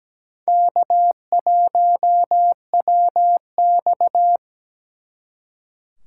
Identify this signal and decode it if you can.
Morse code, heard as a keyed tone, about 20 wpm, 700 Hz. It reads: K1WX
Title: Speed Racing - USA 1x2 Call Signs 17wpm